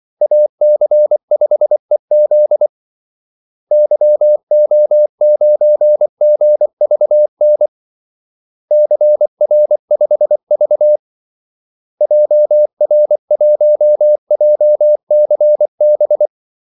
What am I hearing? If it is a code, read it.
Morse code, heard as a keyed tone, about 24 wpm, 600 Hz. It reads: AC5EZ YO9GVN CR5V JR1JCB
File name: Callsigns-24wpm